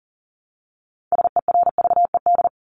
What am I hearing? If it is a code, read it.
Morse code, heard as a keyed tone, about 40 wpm, 700 Hz. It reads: SERVED
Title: ED Suffix - Top 100 Words with Base Words 40wpm